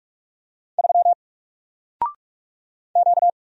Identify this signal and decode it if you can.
Morse code, heard as a keyed tone, about 44 wpm, 700 Hz. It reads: 3 Q
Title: Single Character (Letter:Number:Punctuation:Prosign) 45wpm